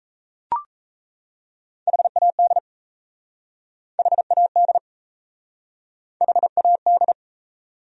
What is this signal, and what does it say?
Morse code, heard as a keyed tone, about 37 wpm, 700 Hz. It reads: HAD HAD HAD